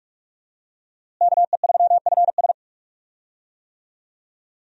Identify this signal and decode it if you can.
Morse code, heard as a keyed tone, about 46 wpm, 700 Hz. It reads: KE3FS
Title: USA Call Signs - 5 Characters - Repeated 2x - 45wpm